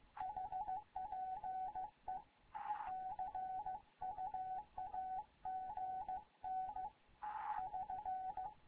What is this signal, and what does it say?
Morse code, heard as a keyed tone, about 15 wpm, 750 Hz, with radio noise. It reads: HPE CUAGN <SN>
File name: Morze-HA2MN / hangp / 2 11 GYAKORLAT